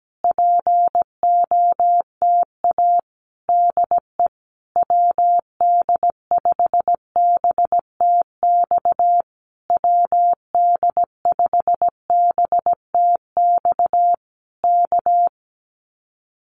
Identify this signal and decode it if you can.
Morse code, heard as a keyed tone, about 17 wpm, 700 Hz. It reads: POTA DE WD5BTX WD5BTX K
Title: Common QSO Phrases Encore 17wpm